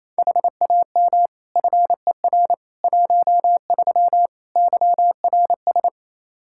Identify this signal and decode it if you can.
Morse code, heard as a keyed tone, about 28 wpm, 700 Hz. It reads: HAM FER 13 YRS
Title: K1USN SST QSO Practice 28wpm